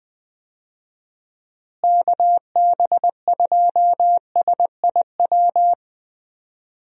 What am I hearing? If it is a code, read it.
Morse code, heard as a keyed tone, about 20 wpm, 700 Hz. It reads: KB2SIW
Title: USA Call Signs - 6 Characters - Repeated 3x - 20wpm